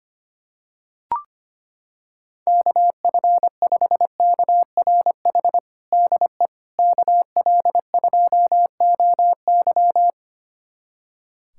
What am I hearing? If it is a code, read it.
Morse code, heard as a keyed tone, about 25 wpm, 700 Hz. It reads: KF5KRH DE KL2OY
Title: USA Callsign DE Callsign 25wpm